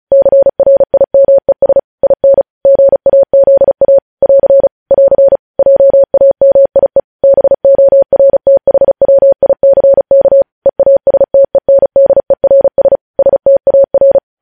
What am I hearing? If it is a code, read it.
Morse code, heard as a keyed tone, about 36 wpm, 550 Hz. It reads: CRIMES IN GAZA <AR> <AR> JAMIE BORTHWICK EASTENDERS STAR
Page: news headlines in morse code at 35 wpm